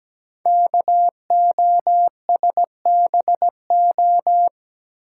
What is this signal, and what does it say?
Morse code, heard as a keyed tone, about 17 wpm, 700 Hz. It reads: KOSBO
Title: Code Groups - 5 Letters - 17wpm